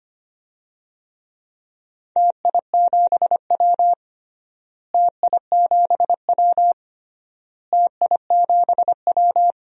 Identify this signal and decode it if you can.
Morse code, heard as a keyed tone, about 25 wpm, 700 Hz. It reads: TI7W TI7W TI7W
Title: DXCC-100 Call Signs - 4 Characters - Repeated 3x - 25wpm